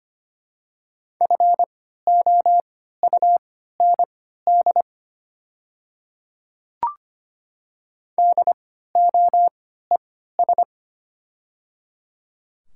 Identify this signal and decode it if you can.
Morse code, heard as a keyed tone, about 25 wpm, spaced as for 14 wpm, 700 Hz. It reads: FOUND DOES
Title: 3x Character Spacing - Top 300 Words - 25wpm